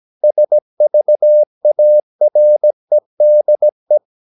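Morse code, heard as a keyed tone, about 17 wpm, 600 Hz. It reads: SVAREDE
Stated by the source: Slow (QRS) Morse practice at 17 wpm, from CW med Gnister.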